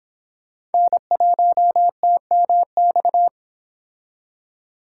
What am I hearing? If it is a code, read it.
Morse code, heard as a keyed tone, about 29 wpm, 700 Hz. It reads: N1TMX